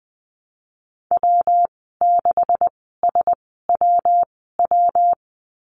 Morse code, heard as a keyed tone, about 20 wpm, spaced as for 15 wpm, 700 Hz. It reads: W6SWW
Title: 186 Lesson 30g - Letter b - Call Signs 20wpm